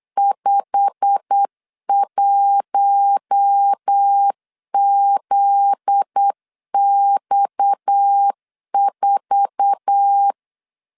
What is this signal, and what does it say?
Morse code, heard as a keyed tone, about 8 wpm, 800 Hz. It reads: 51ZX4
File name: LEZIONI / 8 YJ45 / 8 YJ45 8wpm 3214